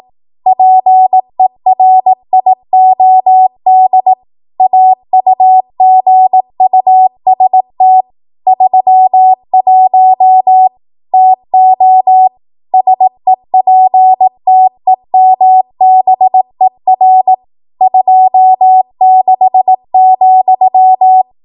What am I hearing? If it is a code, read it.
Morse code, heard as a keyed tone, about 18 wpm, 750 Hz. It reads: PERIOD AUGUST 31 TO SEPTEMBER 26,
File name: Archive / 18 WPM / 18WPM